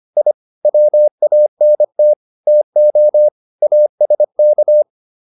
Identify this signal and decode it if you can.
Morse code, heard as a keyed tone, about 25 wpm, 600 Hz. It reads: I WANT TO ASK